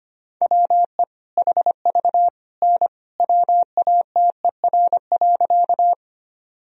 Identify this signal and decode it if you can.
Morse code, heard as a keyed tone, about 25 wpm, 700 Hz. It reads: WE HV N WATER.
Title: Sentences from Top 100 Words and Abbreviations 25wpm